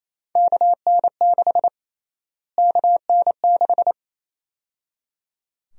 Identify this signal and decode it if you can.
Morse code, heard as a keyed tone, about 28 wpm, 700 Hz. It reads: KN6 KN6
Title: USA Call Sign Prefixes - Repeated 2x - 28wpm